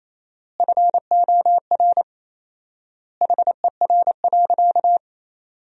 Morse code, heard as a keyed tone, about 28 wpm, 700 Hz. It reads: FOR HER.
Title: 4x Word-Spacing - Sentences from Top 400 Words - 28wpm